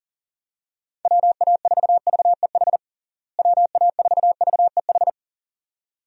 Morse code, heard as a keyed tone, about 40 wpm, 700 Hz. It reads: WA4VEH WA4VEH
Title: USA Call Signs - 6 Characters - Repeated 2x - 40wpm